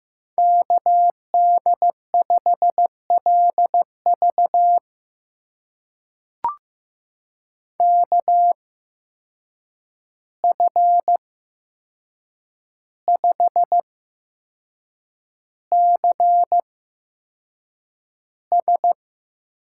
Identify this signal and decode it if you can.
Morse code, heard as a keyed tone, about 15 wpm, spaced as for 6 wpm, 700 Hz. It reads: KD5LV KF5CS
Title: USA Call Signs - 5 Characters - 8x Character Spacing - 15wpm